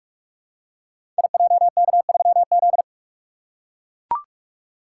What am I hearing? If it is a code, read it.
Morse code, heard as a keyed tone, about 44 wpm, 700 Hz. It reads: IJK3Z
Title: Code Groups - 5 Letter:Numbers - 45wpm